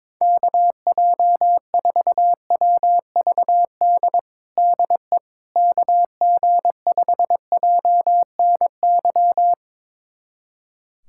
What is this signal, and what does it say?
Morse code, heard as a keyed tone, about 22 wpm, 700 Hz. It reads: KJ4WVD DE KG5JNY
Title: USA Callsign DE Callsign - 6 Characters - 22wpm